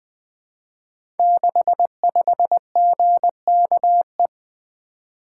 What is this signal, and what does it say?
Morse code, heard as a keyed tone, about 20 wpm, 700 Hz. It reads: 65GKE
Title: Code Groups - 5 Letter:Numbers - 20wpm